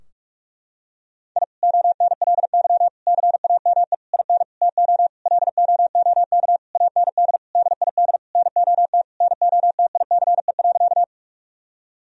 Morse code, heard as a keyed tone, about 46 wpm, 700 Hz. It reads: I ONLY CAME IN TO LOOK AND DID NOT NOTICE.
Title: Sentences from Top 1000 Words - Review Part 2 - 45wpm